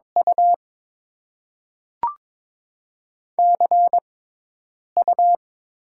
Morse code, heard as a keyed tone, about 22 wpm, spaced as for 8 wpm, 700 Hz. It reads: U CU
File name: Top 20 QSO Elements - 6x Character Spacing - 22wpm